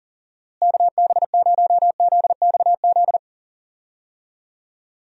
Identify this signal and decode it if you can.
Morse code, heard as a keyed tone, about 40 wpm, 700 Hz. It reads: KB0ZXZ
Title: USA Call Signs - 6 Characters - Repeated 3x - 40wpm